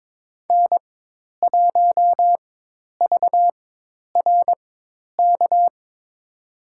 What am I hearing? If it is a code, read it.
Morse code, heard as a keyed tone, about 22 wpm, spaced as for 10 wpm, 700 Hz. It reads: N1VRK
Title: USA Call Signs - 5 Characters - 4x Character Spacing - 22wpm